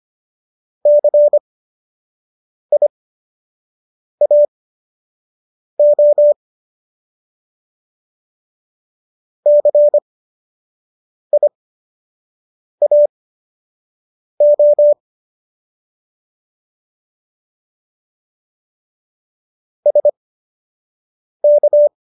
Morse code, heard as a keyed tone, about 25 wpm, spaced as for 6 wpm, 590 Hz. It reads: CIAO CIAO SK